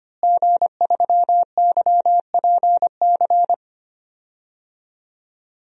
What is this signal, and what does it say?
Morse code, heard as a keyed tone, about 25 wpm, 700 Hz. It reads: G3YPC
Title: DXCC-100 Call Signs - 5 Characters - Repeated 2x - 25wpm